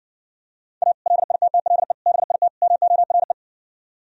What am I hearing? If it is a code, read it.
Morse code, heard as a keyed tone, about 59 wpm, 700 Hz. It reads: A LITTLE BIT MORE